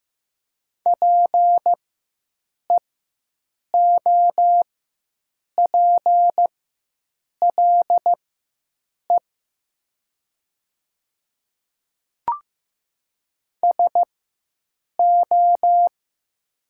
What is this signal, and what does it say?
Morse code, heard as a keyed tone, about 15 wpm, spaced as for 7 wpm, 700 Hz. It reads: PEOPLE SO
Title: 4x Character Spacing - Top 100 Words - 15wpm